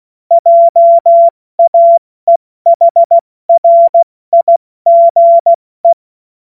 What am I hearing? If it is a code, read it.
Morse code, heard as a keyed tone, about 16 wpm, 680 Hz. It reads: JAEHRIGE
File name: Witz-08-Juli-2025-70-F80